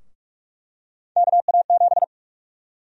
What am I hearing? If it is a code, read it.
Morse code, heard as a keyed tone, about 46 wpm, 700 Hz. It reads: KA7
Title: USA Call Sign Prefixes - 45wpm